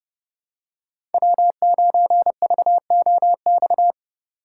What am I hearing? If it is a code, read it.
Morse code, heard as a keyed tone, about 30 wpm, 700 Hz. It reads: W9VOX